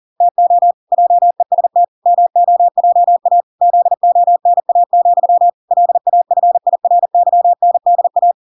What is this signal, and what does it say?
Morse code, heard as a keyed tone, about 40 wpm, 700 Hz. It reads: TO JEST MOJA ZONA, LAFIRYNDA